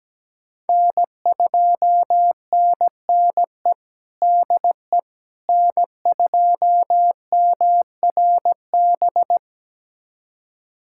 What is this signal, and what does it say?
Morse code, heard as a keyed tone, about 17 wpm, 700 Hz. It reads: N2NNE DE N2MRB
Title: USA Callsign DE Callsign - 5 Characters - 17wpm